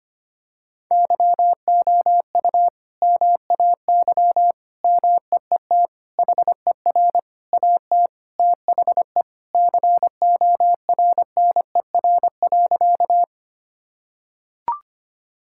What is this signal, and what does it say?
Morse code, heard as a keyed tone, about 25 wpm, 700 Hz. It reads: YOU MAY MEET HER AT THE CORNER.
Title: Sentences from Top 1600 Words 25wpm